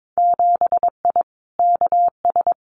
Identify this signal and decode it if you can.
Morse code, heard as a keyed tone, about 22 wpm, 700 Hz. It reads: 7I KS
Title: Winter Field Day Exchanges Encore 22wpm